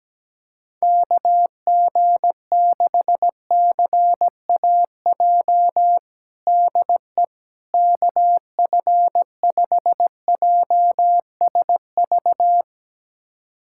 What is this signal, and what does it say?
Morse code, heard as a keyed tone, about 17 wpm, 700 Hz. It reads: KG6CAJ DE KF5JSV
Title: USA Callsign DE Callsign - 6 Characters - 17wpm